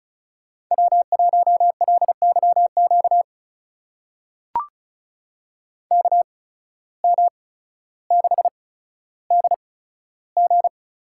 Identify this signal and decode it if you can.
Morse code, heard as a keyed tone, about 36 wpm, spaced as for 15 wpm, 700 Hz. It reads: W1LYQ KM6DG
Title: USA Call Signs - 5 Characters - 8x Character Spacing - 35wpm